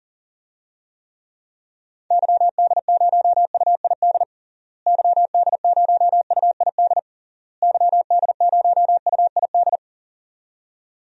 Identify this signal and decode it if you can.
Morse code, heard as a keyed tone, about 37 wpm, 700 Hz. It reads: YD0UID YD0UID YD0UID
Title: DXCC-100 Call Signs - 6 Characters - Repeated 3x - 40wpm